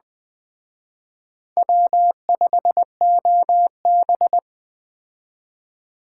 Morse code, heard as a keyed tone, about 20 wpm, 700 Hz. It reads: W5OB
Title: Speed Racing - USA 1x2 Call Signs 17wpm